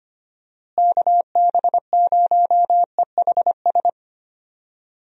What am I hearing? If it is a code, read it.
Morse code, heard as a keyed tone, about 25 wpm, 700 Hz. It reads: KB0EHS
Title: USA Call Signs - 6 Characters - Repeated 2x - 25wpm